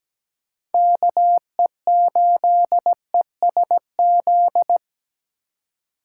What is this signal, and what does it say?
Morse code, heard as a keyed tone, about 17 wpm, 700 Hz. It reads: KE8ESZ
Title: USA Call Signs - 6 Characters - 8x Character Spacing - 17wpm